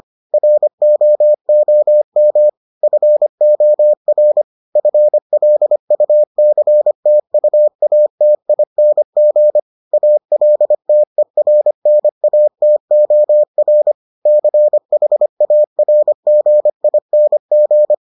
Morse code, heard as a keyed tone, about 25 wpm, 600 Hz. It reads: ROOM FOR FLUCTUATING ALTERNATOR CHARGING